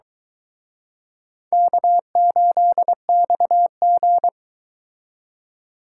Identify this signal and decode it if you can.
Morse code, heard as a keyed tone, about 23 wpm, 700 Hz. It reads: K8XG